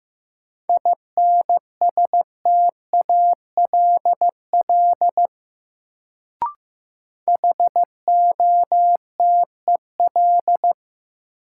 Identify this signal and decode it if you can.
Morse code, heard as a keyed tone, about 15 wpm, 700 Hz. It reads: INSTALL HOTEL
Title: Sentences from Top 1600 Words - New Word Drill Alt - 15wpm